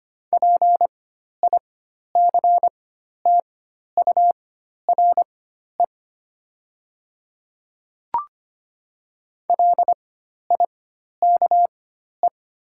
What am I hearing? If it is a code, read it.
Morse code, heard as a keyed tone, about 25 wpm, spaced as for 12 wpm, 700 Hz. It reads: PICTURE LIKE
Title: 4x Character Spacing - Top 200 Words - 25wpm